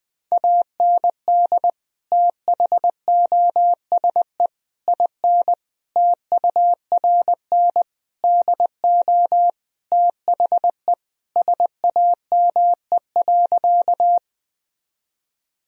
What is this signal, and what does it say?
Morse code, heard as a keyed tone, about 21 wpm, 700 Hz. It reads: AND THOSE IN TURN DO THE SAME.
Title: Sentences from Top 1000 Words - Review Part 1 - 20wpm